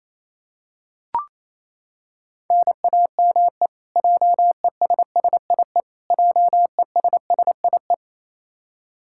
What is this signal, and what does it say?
Morse code, heard as a keyed tone, about 28 wpm, 700 Hz. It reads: NAME JESSIE JESSIE
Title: Practice QSO - Exchange 1 - Beginning - 28wpm